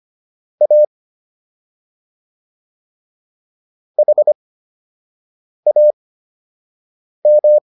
Morse code, heard as a keyed tone, about 25 wpm, spaced as for 6 wpm, 600 Hz. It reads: A HAM